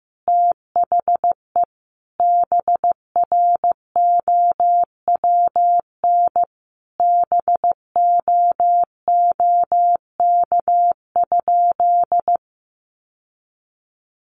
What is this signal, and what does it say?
Morse code, heard as a keyed tone, about 15 wpm, 700 Hz. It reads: THE BROWN BOOK?
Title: Sentences from Top 1200 Words 15wpm